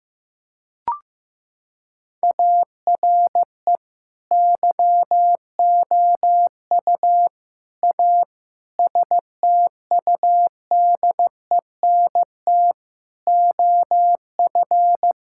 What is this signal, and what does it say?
Morse code, heard as a keyed tone, about 15 wpm, 700 Hz. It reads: ARE YOU A STUDENT OF
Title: Sentences from Top 1000 Words - Review Part 6 - 15wpm